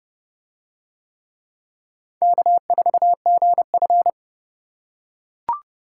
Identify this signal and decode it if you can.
Morse code, heard as a keyed tone, about 26 wpm, 700 Hz. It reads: K4GF